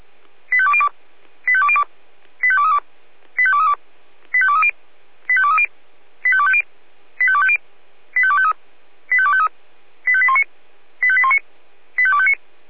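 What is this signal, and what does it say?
Five-tone selective call (ZVEI), heard as quick runs of five tones, tones of about 70 ms. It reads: 85202 85202 85232 85232 85230 85230 85250 85250 85253 85253 85819 85819 85259
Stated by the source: ZVEI five-tone call received by SDR.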